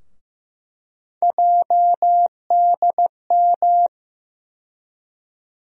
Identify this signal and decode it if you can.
Morse code, heard as a keyed tone, about 15 wpm, 700 Hz. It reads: JDM
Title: USA Call Sign Postfixes - 3 Letters 15wpm